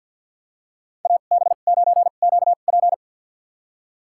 Morse code, heard as a keyed tone, about 51 wpm, spaced as for 36 wpm, 700 Hz. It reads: AB9QP
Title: USA Call Signs - 5 Characters - 2x Character Spacing - 50wpm